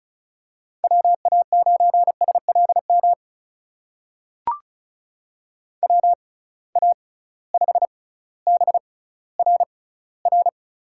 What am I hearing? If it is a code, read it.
Morse code, heard as a keyed tone, about 36 wpm, spaced as for 18 wpm, 700 Hz. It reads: WA9SLM WA5BRR
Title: USA Call Signs - 6 Characters - 6x Character Spacing - 35wpm